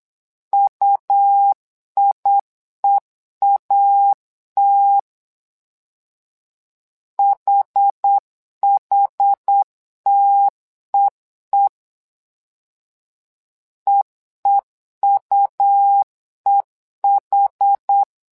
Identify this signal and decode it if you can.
Morse code, heard as a keyed tone, about 8 wpm, 800 Hz. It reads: UIEAT HHTEE EEUEH
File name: LEZIONI / 2 UHDN / 2 UH 8wpm 2129